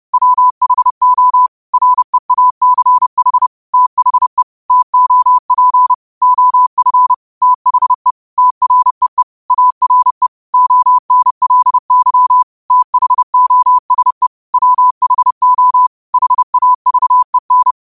Morse code, heard as a keyed tone, about 30 wpm, 1000 Hz. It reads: WHO REACH THE TOP OF THE TREE ARE ONLY THOSE WHO HAVEN
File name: qotdfiles / 30wpm